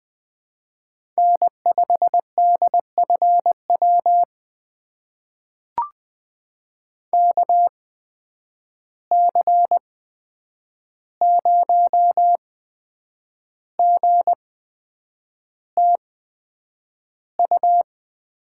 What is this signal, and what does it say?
Morse code, heard as a keyed tone, about 20 wpm, spaced as for 8 wpm, 700 Hz. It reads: N5DFW KC0GTU
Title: USA Call Signs - 8x Character Spacing - 20wpm